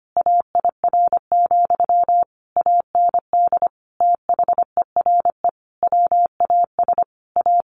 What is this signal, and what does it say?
Morse code, heard as a keyed tone, about 25 wpm, 700 Hz. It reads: AIR, AND THERE WAS A
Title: A Sound Of Thunder 25wpm